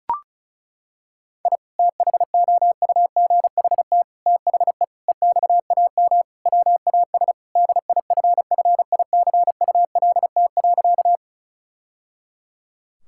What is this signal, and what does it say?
Morse code, heard as a keyed tone, about 36 wpm, 700 Hz. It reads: I THOUGHT THE EXAM WAS DIFFICULT.
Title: Sentences from Top 1000 Words - Review Part 5 - 35wpm